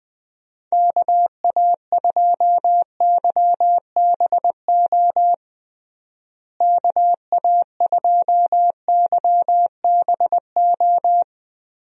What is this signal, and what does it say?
Morse code, heard as a keyed tone, about 20 wpm, 700 Hz. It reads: KA2YBO KA2YBO
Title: USA Call Signs - 6 Characters - Repeated 2x - 20wpm